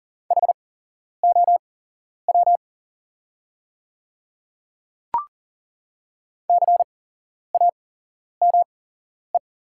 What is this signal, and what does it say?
Morse code, heard as a keyed tone, about 40 wpm, spaced as for 11 wpm, 690 Hz. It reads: HOW CAME
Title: 8x Character Spacing - Top 300 Words - 40wpm